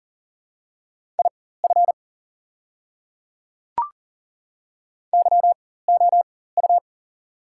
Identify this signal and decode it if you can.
Morse code, heard as a keyed tone, about 39 wpm, spaced as for 18 wpm, 690 Hz. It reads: IF YOU
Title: 4x Character Spacing - Top 300 Words - 40wpm